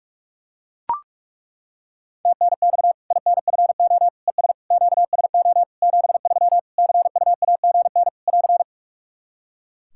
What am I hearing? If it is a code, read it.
Morse code, heard as a keyed tone, about 41 wpm, 700 Hz. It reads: TNX INFO ES QSO 73 CUAGN <AR>
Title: Practice QSO - Exchange 4 - Ending - 45wpm